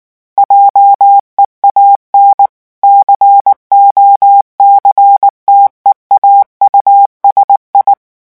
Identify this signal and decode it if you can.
Morse code, heard as a keyed tone, about 19 wpm, 800 Hz. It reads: JEAN COCTEAU SI
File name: morse mp3 / revision9 citations humoristiques vitesse 18 mots minute